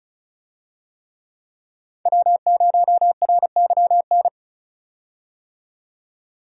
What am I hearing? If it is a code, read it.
Morse code, heard as a keyed tone, about 34 wpm, 700 Hz. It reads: W0RYN
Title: USA Call Signs - 5 Characters - 35wpm